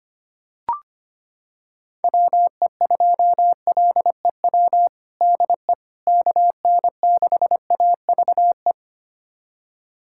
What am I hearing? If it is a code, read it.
Morse code, heard as a keyed tone, about 25 wpm, 700 Hz. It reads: WE2LEW DE KN6AVE
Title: USA Callsign DE Callsign - 6 Characters - 25wpm